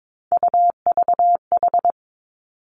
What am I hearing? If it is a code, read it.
Morse code, heard as a keyed tone, about 22 wpm, 700 Hz. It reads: UVH
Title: Three Challenging Letter-Number Combinations 22wpm